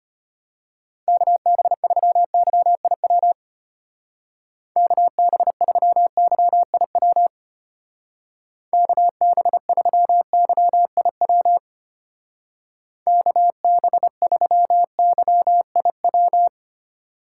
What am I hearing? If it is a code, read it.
Morse code, heard as a keyed tone, about 32 wpm, 700 Hz. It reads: KB3YIW KB3YIW KB3YIW KB3YIW